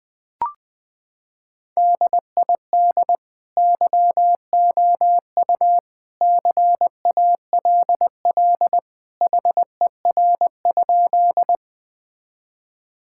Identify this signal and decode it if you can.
Morse code, heard as a keyed tone, about 20 wpm, 700 Hz. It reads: DID YOU CALL HER?
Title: Sentences with 2-Word Phrases - 20wpm